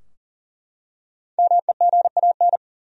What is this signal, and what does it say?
Morse code, heard as a keyed tone, about 40 wpm, 700 Hz. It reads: MEGAN
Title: USA Common Names - 40wpm